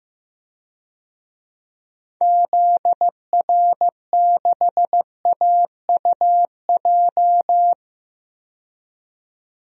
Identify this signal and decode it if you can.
Morse code, heard as a keyed tone, about 15 wpm, 700 Hz. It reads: ZR6AUJ
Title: DXCC-100 Call Signs - 6 Characters - 15wpm